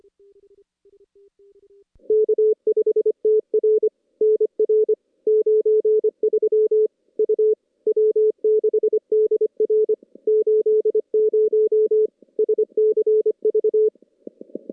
Morse code, heard as a keyed tone, about 25 wpm, 410 Hz. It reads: K5TR NR 93 U W6DR 80 SCV
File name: contests / CW-SS-k5tr- -2300